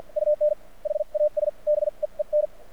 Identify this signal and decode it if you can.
Morse code, heard as a keyed tone, about 52 wpm, spaced as for 42 wpm, 600 Hz. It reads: UN HAS BEEN
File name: archived / CRSnet